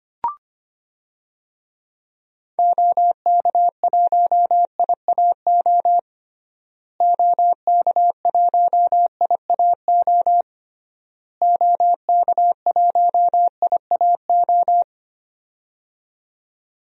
Morse code, heard as a keyed tone, about 25 wpm, 700 Hz. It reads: OK1IAO OK1IAO OK1IAO
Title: DXCC-100 Call Signs - 6 Characters - Repeated 3x - 25wpm